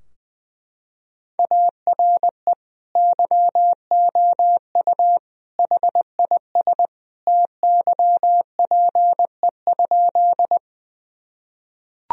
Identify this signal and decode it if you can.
Morse code, heard as a keyed tone, about 20 wpm, 700 Hz. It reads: ARE YOU HIS TYPE?
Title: Sentences from Top 1000 Words - Review Part 3 - 20wpm